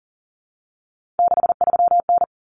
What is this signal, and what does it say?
Morse code, heard as a keyed tone, about 40 wpm, 700 Hz. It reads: 63N
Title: Common Cut Numbers - 3-Digit 40wpm